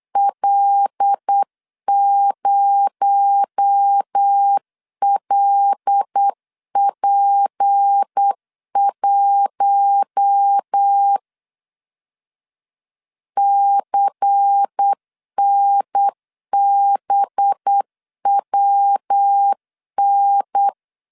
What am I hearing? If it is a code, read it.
Morse code, heard as a keyed tone, about 8 wpm, 800 Hz. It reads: L0LP1 CNBWN
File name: CW / DET / 1 6 1500C S C 8wpm 5219